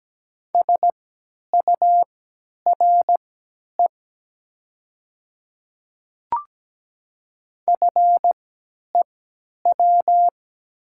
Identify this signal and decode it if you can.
Morse code, heard as a keyed tone, about 17 wpm, spaced as for 10 wpm, 700 Hz. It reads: SURE FEW
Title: 3x Character Spacing - Top 300 Words - 17wpm